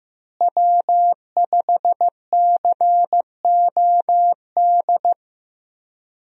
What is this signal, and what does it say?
Morse code, heard as a keyed tone, about 15 wpm, 700 Hz. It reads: W5COD
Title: USA Call Signs - 5 Characters - Repeated 3x - 15wpm